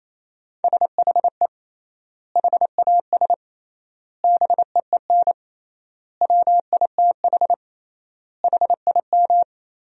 Morse code, heard as a keyed tone, about 28 wpm, 700 Hz. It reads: SHE HAS BEEN WITH HIM
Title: 3x Word-Spacing - Sentences from Top 200 Words - 28wpm